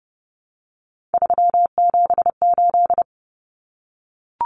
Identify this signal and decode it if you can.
Morse code, heard as a keyed tone, about 30 wpm, 700 Hz. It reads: 378
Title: Common Cut Numbers - 3-Digit 30wpm